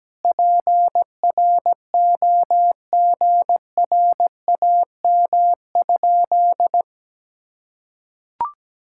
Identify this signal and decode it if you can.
Morse code, heard as a keyed tone, about 17 wpm, 700 Hz. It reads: PROGRAM?
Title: Sentences from Top 1000 Words - Review Part 5 - 17wpm